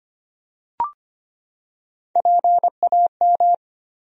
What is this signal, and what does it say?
Morse code, heard as a keyed tone, about 25 wpm, 700 Hz. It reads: PAM
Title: USA Common Names - Max 4 Letters - 25wpm